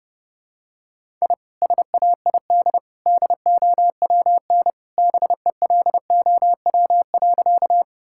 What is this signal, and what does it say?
Morse code, heard as a keyed tone, about 31 wpm, 700 Hz. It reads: I SAID DOWN BELOW.
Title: Sentences from Top 1000 Words - Review Part 5 - 30wpm